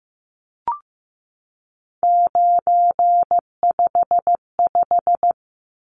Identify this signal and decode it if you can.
Morse code, heard as a keyed tone, about 15 wpm, 700 Hz. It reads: 955
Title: Numbers - 3-Digit 15wpm